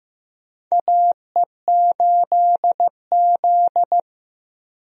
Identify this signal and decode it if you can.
Morse code, heard as a keyed tone, about 15 wpm, 700 Hz. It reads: AE8Z
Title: USA Call Signs - 4 Characters - Repeated 3x - 15wpm